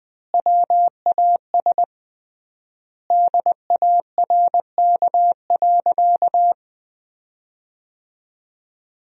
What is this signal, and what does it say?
Morse code, heard as a keyed tone, about 20 wpm, 700 Hz. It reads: WAS DARK.
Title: 3x Word-Spacing - Sentences from Top 500 Words - 20wpm